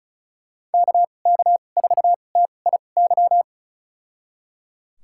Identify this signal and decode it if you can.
Morse code, heard as a keyed tone, about 36 wpm, spaced as for 25 wpm, 700 Hz. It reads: KK4TIY
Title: USA Call Signs - 6 Characters - 2x Character Spacing - 35wpm